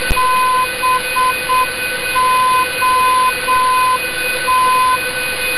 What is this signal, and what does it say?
Morse code, heard as a keyed tone, about 7 wpm, 1020 Hz, with noise. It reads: BOT